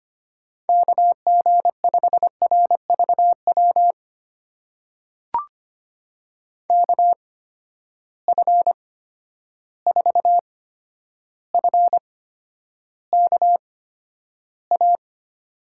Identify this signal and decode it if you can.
Morse code, heard as a keyed tone, about 25 wpm, spaced as for 11 wpm, 700 Hz. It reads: KG5RVW KF4FKA
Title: USA Call Signs - 6 Characters - 8x Character Spacing - 25wpm